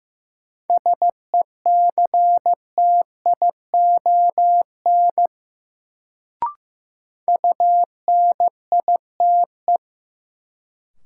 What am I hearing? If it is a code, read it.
Morse code, heard as a keyed tone, about 16 wpm, 700 Hz. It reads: SECTION UNITE
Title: Sentences from Top 1000 Words - Review Part 5 - New Word Drill - 15wpm